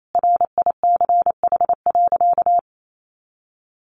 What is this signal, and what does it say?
Morse code, heard as a keyed tone, about 28 wpm, 700 Hz. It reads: RICH.
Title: Sentences from Top 1100 Words 28wpm